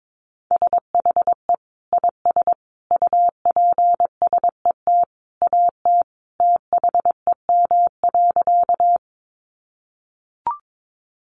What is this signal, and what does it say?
Morse code, heard as a keyed tone, about 22 wpm, 700 Hz. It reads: SHE IS UPSET AT THEM.
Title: Sentences from Top 800 Words 22wpm